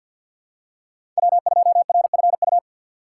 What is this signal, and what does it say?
Morse code, heard as a keyed tone, about 52 wpm, 700 Hz. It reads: W2RFU
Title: USA Call Signs - 5 Characters - 2x Character Spacing - 50wpm